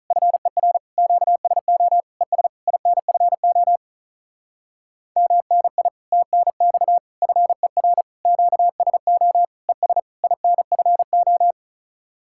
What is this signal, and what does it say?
Morse code, heard as a keyed tone, about 43 wpm, 700 Hz. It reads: FER QSO ES INFO MNI TNX FER QSO ES INFO
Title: Speed-Racing Common QSO Phrases - 35wpm